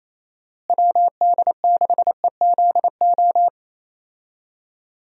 Morse code, heard as a keyed tone, about 28 wpm, 700 Hz. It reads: WD6EZO
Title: USA Call Signs - 6 Characters - Repeated 3x - 28wpm